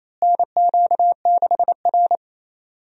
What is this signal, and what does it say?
Morse code, heard as a keyed tone, about 28 wpm, 700 Hz. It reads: NQ6R
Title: USA Call Signs - 4 Characters - Repeated 2x - 28wpm